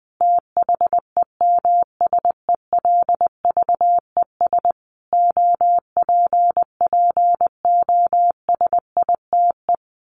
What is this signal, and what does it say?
Morse code, heard as a keyed tone, about 20 wpm, 700 Hz. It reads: THEMSELVES OPPOSITE
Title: A Sound Of Thunder 20wpm